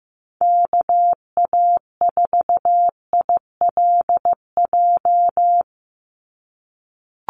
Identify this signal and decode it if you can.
Morse code, heard as a keyed tone, about 15 wpm, 700 Hz. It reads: KA4ILJ
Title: USA Call Signs 15wpm